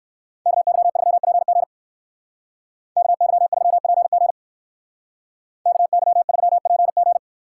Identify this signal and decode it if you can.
Morse code, heard as a keyed tone, about 58 wpm, 700 Hz. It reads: KY3PG KY3PG KY3PG